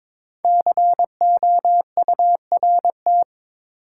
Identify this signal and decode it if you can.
Morse code, heard as a keyed tone, about 22 wpm, 700 Hz. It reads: COURT
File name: 5 Letter Words - First 3 Letters Given - 22wpm